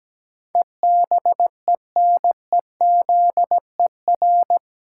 Morse code, heard as a keyed tone, about 17 wpm, 700 Hz. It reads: EBENEZER
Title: Pangram Challenge v2 - 17wpm